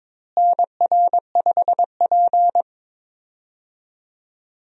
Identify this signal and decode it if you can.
Morse code, heard as a keyed tone, about 23 wpm, 700 Hz. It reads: NR5P
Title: USA Call Signs - 4 Characters - Repeated 3x - 22wpm